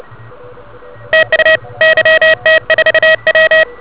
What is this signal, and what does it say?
Morse code, heard as a keyed tone, about 31 wpm, 1900 Hz, with noise. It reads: TU YT4W